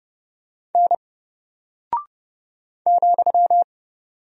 Morse code, heard as a keyed tone, about 30 wpm, 700 Hz. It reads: N ,
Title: Single Character (Letter:Number:Punctuation:Prosign) Alt 30wpm